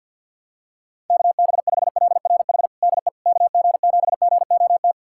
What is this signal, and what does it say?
Morse code, heard as a keyed tone, about 52 wpm, 700 Hz. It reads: KB5LRH DE KG7GOT
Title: USA Callsign DE Callsign 50wpm